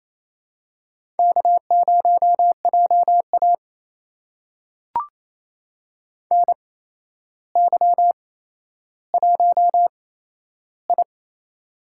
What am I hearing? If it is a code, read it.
Morse code, heard as a keyed tone, about 28 wpm, spaced as for 12 wpm, 700 Hz. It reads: K0JA NY1I
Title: USA Call Signs - 4 Characters - 8x Character Spacing - 28wpm